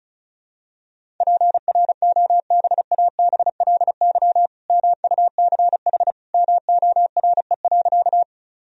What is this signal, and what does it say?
Morse code, heard as a keyed tone, about 36 wpm, 700 Hz. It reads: PROBABLY MUCH MORE.
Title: Sentences from Top 1000 Words - Review Part 4 - 35wpm